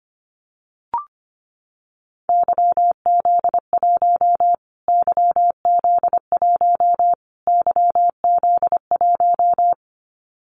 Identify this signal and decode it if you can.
Morse code, heard as a keyed tone, about 25 wpm, 700 Hz. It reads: YZ1 YZ1 YZ1
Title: Callsign Prefixes for Top 25 DXCC - Repeated 3x - 25wpm